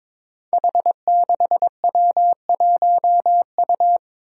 Morse code, heard as a keyed tone, about 22 wpm, 700 Hz. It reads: H6W1U
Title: Five Challenging Letter-Number Combinations 22wpm